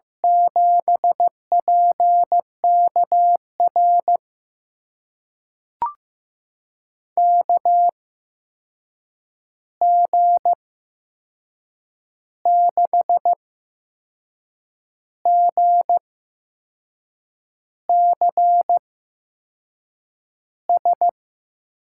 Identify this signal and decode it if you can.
Morse code, heard as a keyed tone, about 15 wpm, spaced as for 6 wpm, 700 Hz. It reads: 7PKR KG6GCS
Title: USA Call Signs - 6 Characters - 8x Character Spacing - 15wpm